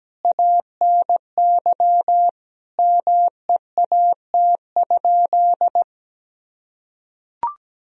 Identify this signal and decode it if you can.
Morse code, heard as a keyed tone, about 17 wpm, 700 Hz. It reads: ANY MEAT?
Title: Sentences from Top 1000 Words - Review Part 3 - 17wpm